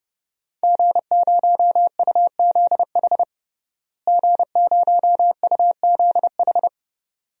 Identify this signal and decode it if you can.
Morse code, heard as a keyed tone, about 30 wpm, 700 Hz. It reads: G0UZH G0UZH
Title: DXCC-100 Call Signs - 5 Characters - Repeated 2x - 30wpm